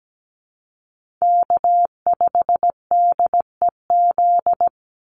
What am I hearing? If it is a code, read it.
Morse code, heard as a keyed tone, about 17 wpm, 700 Hz. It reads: K5DEZ
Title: NA SOTA Chaser Callsigns 17wpm